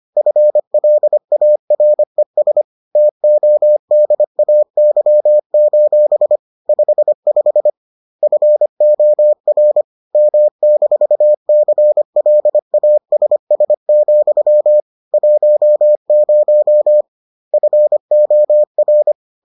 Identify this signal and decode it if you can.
Morse code, heard as a keyed tone, about 25 wpm, 600 Hz. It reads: FLARES TODAY: 55 FOR M-CLASS, 10 FOR